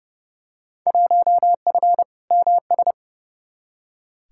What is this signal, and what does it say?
Morse code, heard as a keyed tone, about 30 wpm, 700 Hz. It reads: 1F MS
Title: Field Day Exchanges Encore - 30wpm